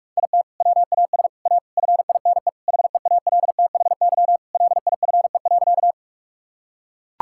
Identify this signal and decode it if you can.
Morse code, heard as a keyed tone, about 43 wpm, 700 Hz. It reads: IT WAS A FINE HEALTHY LIFE.
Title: Sentences from Top 1000 Words - Review Part 5 - 45wpm